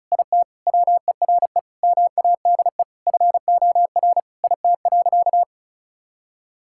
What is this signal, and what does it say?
Morse code, heard as a keyed tone, about 34 wpm, 700 Hz. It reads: IT WERE MADE FOR IT.
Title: Sentences from Top 1600 Words 35wpm